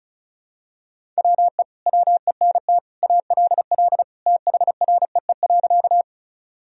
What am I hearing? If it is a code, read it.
Morse code, heard as a keyed tone, about 36 wpm, 700 Hz. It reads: WE WENT ALL THREE.
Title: Sentences from Top 1000 Words - Review Part 1 - 35wpm